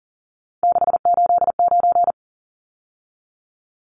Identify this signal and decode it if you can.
Morse code, heard as a keyed tone, about 40 wpm, 700 Hz. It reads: 689
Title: Numbers - 3-Digit 40wpm